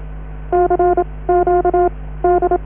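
Morse code, heard as a keyed tone, about 27 wpm, spaced as for 18 wpm, 720 Hz, with noise. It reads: CQD
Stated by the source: Norddeich Radio (DAN) coast station Morse on 500 kHz.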